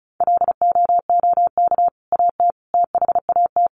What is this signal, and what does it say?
Morse code, heard as a keyed tone, about 34 wpm, 700 Hz. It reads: LOOK AT THAT
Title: Sentences from Top 1100 Words 35wpm